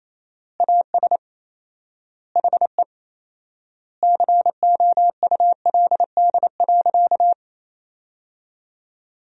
Morse code, heard as a keyed tone, about 27 wpm, 700 Hz. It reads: AS HE COULD.
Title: 4x Word-Spacing - Sentences from Top 500 Words - 28wpm